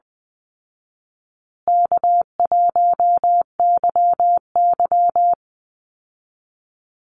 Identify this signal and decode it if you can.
Morse code, heard as a keyed tone, about 20 wpm, 700 Hz. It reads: K1YY
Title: Easy Call Signs Repeated 3x - 20wpm